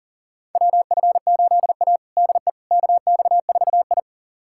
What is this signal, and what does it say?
Morse code, heard as a keyed tone, about 40 wpm, 700 Hz. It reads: WF8A DE KX4I
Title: USA Callsign DE Callsign - 4 Characters - 40wpm